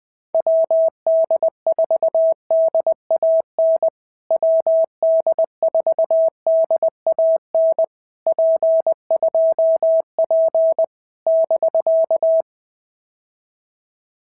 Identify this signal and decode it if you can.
Morse code, heard as a keyed tone, about 20 wpm, 650 Hz. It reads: WD4DAN WD4DAN P2P <BK>
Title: POTA QSO Practice - 20wpm